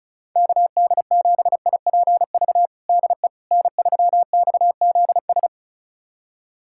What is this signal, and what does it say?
Morse code, heard as a keyed tone, about 34 wpm, 700 Hz. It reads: KD7IPV DE N3XZS
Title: USA Callsign DE Callsign 35wpm